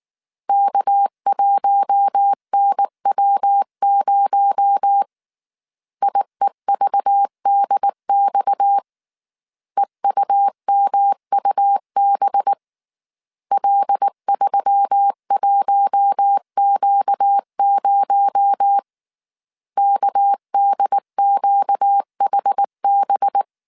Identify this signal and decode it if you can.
Morse code, heard as a keyed tone, about 19 wpm, 800 Hz. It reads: K1NW0 IEVDX EUMUB L31Q0 KDQHB